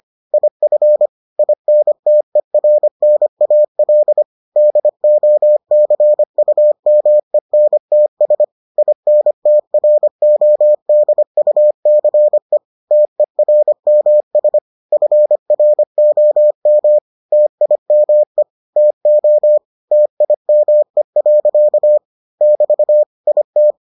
Morse code, heard as a keyed tone, about 25 wpm, 600 Hz. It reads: IF INTERNAL DOCUMENTS INTRODUCE TERMS FROM TIME TO TIME. <BT> IT